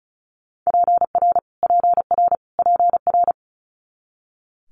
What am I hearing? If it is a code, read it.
Morse code, heard as a keyed tone, about 36 wpm, 700 Hz. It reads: PR PR PR
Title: Multi-letter Phonemes Repeated 3x 35wpm